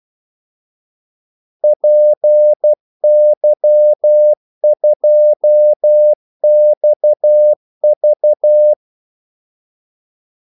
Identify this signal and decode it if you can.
Morse code, heard as a keyed tone, about 12 wpm, 600 Hz. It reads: PY2XV
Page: 2 ekstra ordmellemrum mellem kaldesignalerne